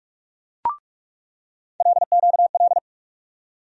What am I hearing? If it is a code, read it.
Morse code, heard as a keyed tone, about 44 wpm, 700 Hz. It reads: LQL
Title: USA Call Sign Postfixes - 3 Letters 45wpm